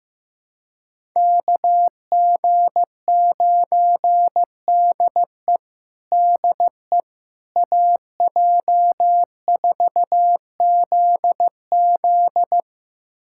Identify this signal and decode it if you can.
Morse code, heard as a keyed tone, about 15 wpm, 700 Hz. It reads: KG9DE DE AJ4ZZ
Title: USA Callsign DE Callsign - 5 Characters - 15wpm